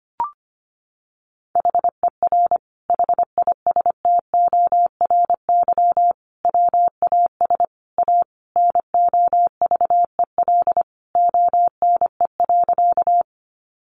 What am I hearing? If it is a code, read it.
Morse code, heard as a keyed tone, about 25 wpm, 700 Hz. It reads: HER HISTORY WAS A NOVEL ONE.
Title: Sentences from Top 1300 Words 25wpm